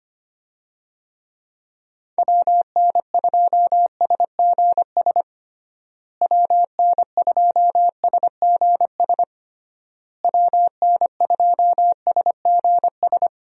Morse code, heard as a keyed tone, about 25 wpm, 700 Hz. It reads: WN2SGS WN2SGS WN2SGS
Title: USA Call Signs - 6 Characters - Repeated 3x - 25wpm